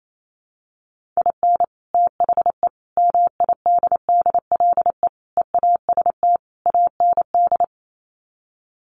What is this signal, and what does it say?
Morse code, heard as a keyed tone, about 28 wpm, 700 Hz. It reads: IN THE MIDDLE EAST AND
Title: Sets of 5 Words Encore 28wpm